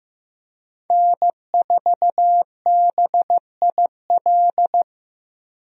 Morse code, heard as a keyed tone, about 15 wpm, 700 Hz. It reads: N4BIL
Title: USA Call Signs - 5 Characters - Repeated 2x - 15wpm